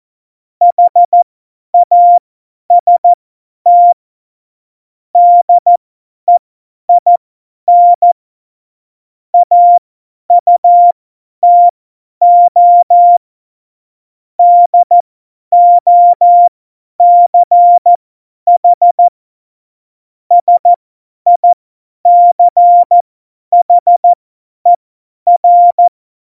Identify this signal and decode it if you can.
Morse code, heard as a keyed tone, about 14 wpm, spaced as for 10 wpm, 700 Hz. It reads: HAST DEIN AUTO DOCH SICHER
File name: Morsekurs / Witze / Witz 21 April 2020, 50 F70